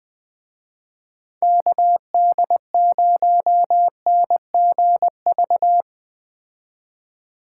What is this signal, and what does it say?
Morse code, heard as a keyed tone, about 20 wpm, 700 Hz. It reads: KD0NGV
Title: Speed Racing - USA 2x3 Call Signs 17wpm